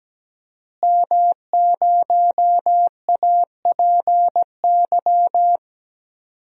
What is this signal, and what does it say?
Morse code, heard as a keyed tone, about 17 wpm, 700 Hz. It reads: M0APY
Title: DXCC-100 Call Signs - 5 Characters - Repeated 2x - 17wpm